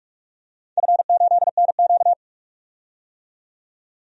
Morse code, heard as a keyed tone, about 46 wpm, 700 Hz. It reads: F8NQ
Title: DXCC-100 Call Signs - 4 Characters - 45wpm